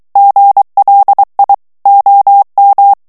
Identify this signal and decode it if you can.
Morse code, heard as a keyed tone, about 23 wpm, 800 Hz. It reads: GLI OM
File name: morse mp3 / ITALIEN texte clair 1 vitesse 22 mots minute